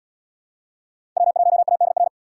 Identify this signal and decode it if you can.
Morse code, heard as a keyed tone, about 72 wpm, 700 Hz. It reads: W1ADR